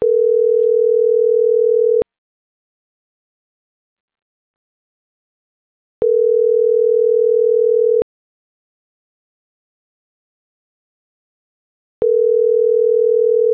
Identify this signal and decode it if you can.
Ringback tone (North America), 440 + 480 Hz.